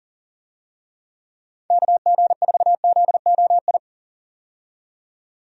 Morse code, heard as a keyed tone, about 40 wpm, 700 Hz. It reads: KG4ZOI
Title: USA Call Signs - 6 Characters - 40wpm